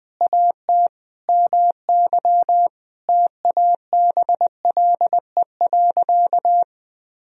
Morse code, heard as a keyed tone, about 20 wpm, 700 Hz. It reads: AT MY TABLE.
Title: Sentences from Top 1000 Words - Review Part 2 - 20wpm